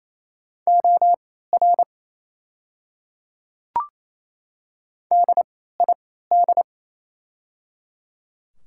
Morse code, heard as a keyed tone, about 28 wpm, spaced as for 16 wpm, 700 Hz. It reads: OR DID
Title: 3x Character Spacing - Top 100 Words - 28wpm